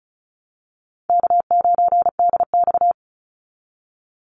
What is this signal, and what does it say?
Morse code, heard as a keyed tone, about 34 wpm, 700 Hz. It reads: K9DX
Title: Easy Call Signs Repeated 2x - 35wpm